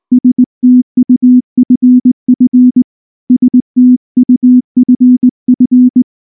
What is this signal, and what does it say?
Morse code, heard as a keyed tone, about 19 wpm, 250 Hz. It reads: STUFF STUFF